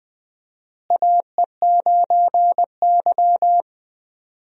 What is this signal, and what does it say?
Morse code, heard as a keyed tone, about 20 wpm, 700 Hz. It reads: AE9Y
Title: USA Call Signs - 4 Characters - Repeated 2x - 20wpm